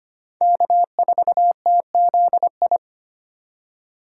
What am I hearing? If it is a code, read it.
Morse code, heard as a keyed tone, about 32 wpm, 700 Hz. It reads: K4TZI